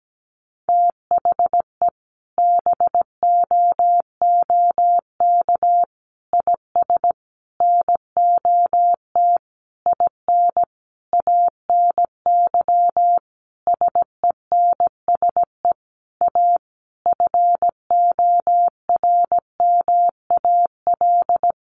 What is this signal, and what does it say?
Morse code, heard as a keyed tone, about 17 wpm, 700 Hz. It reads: THE BOOK IS NOT IN ANY SENSE A FORMAL
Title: Sentences from Top 1300 Words 17wpm